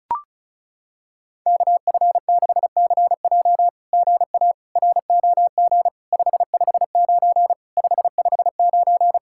Morse code, heard as a keyed tone, about 35 wpm, 700 Hz. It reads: KF6CJ GA ROG 559 559
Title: SOTA QSO Practice 35wpm